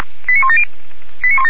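Five-tone selective call (ZVEI), heard as quick runs of five tones, tones of about 70 ms. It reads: 89170
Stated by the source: Five-tone ZVEI selective call 89136 received on an SDR, from feuerwehr-orlamuende.org fire brigade server.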